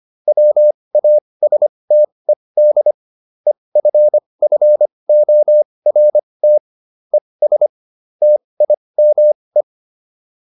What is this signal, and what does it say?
Morse code, heard as a keyed tone, about 25 wpm, spaced as for 20 wpm, 600 Hz. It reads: WASTED EFFORT ES TIME